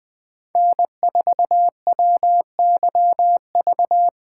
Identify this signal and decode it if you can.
Morse code, heard as a keyed tone, about 20 wpm, 700 Hz. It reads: N4WYV
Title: Speed Racing - USA Call Signs 17wpm